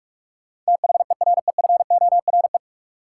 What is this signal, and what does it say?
Morse code, heard as a keyed tone, about 46 wpm, 700 Hz. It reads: THEREFORE
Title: Sentences from Top 800 Words - New Word Drill - 45wpm